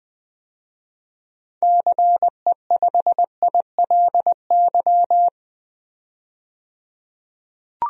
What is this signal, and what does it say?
Morse code, heard as a keyed tone, about 20 wpm, 700 Hz. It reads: CE5ILY
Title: DXCC-100 Call Signs - 6 Characters - 20wpm